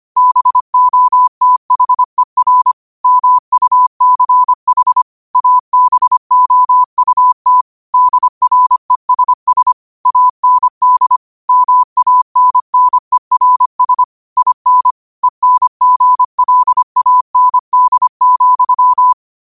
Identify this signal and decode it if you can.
Morse code, heard as a keyed tone, about 25 wpm, 1000 Hz. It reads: BOTHER MUCH ABOUT DRESS AND MANNERS IN ENGLAND,
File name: qotdfiles / 25wpm